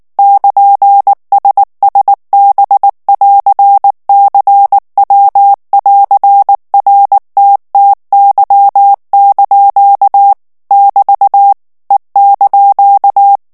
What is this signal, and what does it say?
Morse code, heard as a keyed tone, about 19 wpm, 800 Hz. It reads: <KN>SSB<AR>CW<AR>RTTY) <BT> E)
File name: morse mp3 / ITALIEN texte clair 2 vitesse 18 mots minute